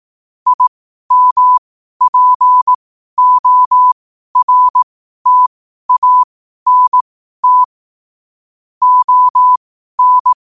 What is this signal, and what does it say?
Morse code, heard as a keyed tone, about 18 wpm, spaced as for 12 wpm, 1000 Hz. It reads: IMPORTANT ON
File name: qotdfiles / 12wpm